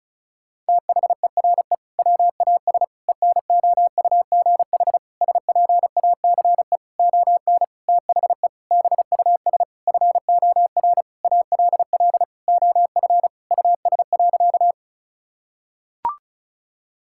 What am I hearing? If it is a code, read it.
Morse code, heard as a keyed tone, about 36 wpm, 700 Hz. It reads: THERE WAS ENOUGH SPACE ON THE BUS FOR ALL OF US.
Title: Sentences from Top 1000 Words - Review Part 6 - 35wpm